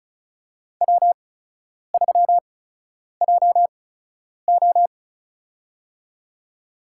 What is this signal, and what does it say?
Morse code, heard as a keyed tone, about 34 wpm, spaced as for 10 wpm, 700 Hz. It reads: W3JO
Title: USA Call Signs - 4 Characters - 8x Character Spacing - 35wpm